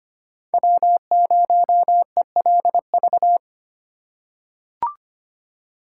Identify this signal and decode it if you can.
Morse code, heard as a keyed tone, about 25 wpm, 700 Hz. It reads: W0ELV
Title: Code Groups - 5 Letter:Numbers - 25wpm